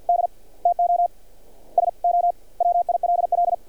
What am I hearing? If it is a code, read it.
Morse code, heard as a keyed tone, about 46 wpm, 700 Hz. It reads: D TO SO WILL
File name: archived / CRSnet